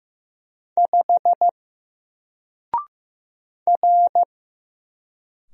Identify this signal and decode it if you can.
Morse code, heard as a keyed tone, about 15 wpm, 700 Hz. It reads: 5 R
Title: Single Character (Letter:Number:Punctuation:Prosign) 15wpm